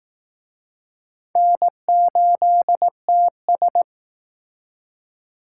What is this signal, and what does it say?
Morse code, heard as a keyed tone, about 19 wpm, 700 Hz. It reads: N8TS